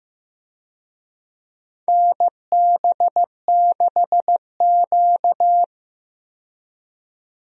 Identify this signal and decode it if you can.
Morse code, heard as a keyed tone, about 15 wpm, 700 Hz. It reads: NB6Q
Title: USA Call Signs - 4 Characters - 15wpm